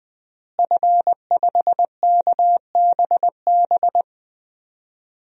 Morse code, heard as a keyed tone, about 20 wpm, 700 Hz. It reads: F5KBB
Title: DXCC-100 Call Signs - 5 Characters - Repeated 3x - 20wpm